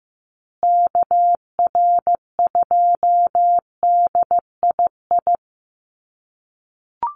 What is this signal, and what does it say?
Morse code, heard as a keyed tone, about 15 wpm, 700 Hz. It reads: KR2DII
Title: USA Call Signs Repeated 2x 15wpm